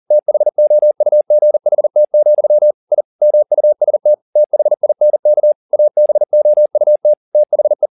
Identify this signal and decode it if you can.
Morse code, heard as a keyed tone, about 40 wpm, 600 Hz. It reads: THOUGHT, I MUST THINK ABOUT THE
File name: the old man and the sea 2 40wpm